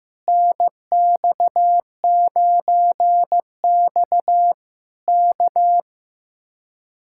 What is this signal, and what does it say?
Morse code, heard as a keyed tone, about 15 wpm, 700 Hz. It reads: NX9X K
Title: Speed-Racing Common QSO Phrases Encore - 15wpm